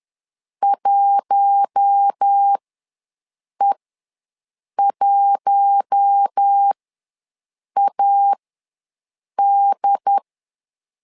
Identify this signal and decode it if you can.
Morse code, heard as a keyed tone, about 10 wpm, spaced as for 6 wpm, 800 Hz. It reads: 1E1AD
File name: LEZIONI / 4 Q1 ws 10wpm 2538